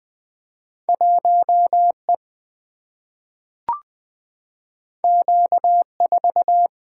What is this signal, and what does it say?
Morse code, heard as a keyed tone, about 20 wpm, 700 Hz. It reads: 1E Q4
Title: Single-Letter Single-Number 20wpm